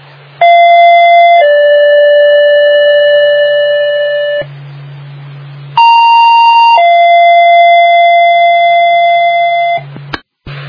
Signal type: two-tone page